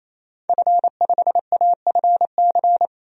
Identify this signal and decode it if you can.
Morse code, heard as a keyed tone, about 28 wpm, 700 Hz. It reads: F5AFC
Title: DXCC-100 Call Signs - 5 Characters - Repeated 3x - 28wpm